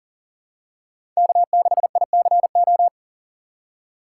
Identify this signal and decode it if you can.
Morse code, heard as a keyed tone, about 40 wpm, 700 Hz. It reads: K6ICO
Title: USA Call Signs - 6x Character Spacing - 40wpm